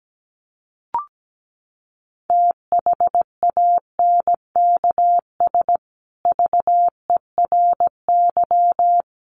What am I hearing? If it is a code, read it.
Morse code, heard as a keyed tone, about 17 wpm, 700 Hz. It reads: THANKS VERY
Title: Sets of 5 Words Encore 17wpm